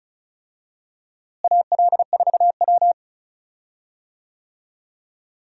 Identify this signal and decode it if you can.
Morse code, heard as a keyed tone, about 34 wpm, 700 Hz. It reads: AL4W
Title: USA Call Signs - 4 Characters - Repeated 2x - 35wpm